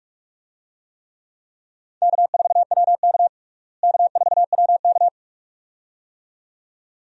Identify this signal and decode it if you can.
Morse code, heard as a keyed tone, about 46 wpm, 700 Hz. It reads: K4WK K4WK
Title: USA Call Signs - 4 Characters - Repeated 2x - 45wpm